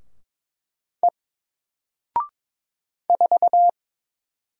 Morse code, heard as a keyed tone, about 22 wpm, 700 Hz. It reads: E 4
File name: Single-Letter-Number-22wpm